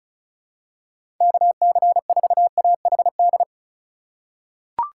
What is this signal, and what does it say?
Morse code, heard as a keyed tone, about 34 wpm, 700 Hz. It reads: KC4AHD
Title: USA Call Signs - 6 Characters - 4x Character Spacing - 35wpm